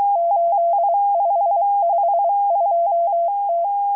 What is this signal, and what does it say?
Morse code, heard as a keyed tone, about 23 wpm, 680 Hz. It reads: 8552T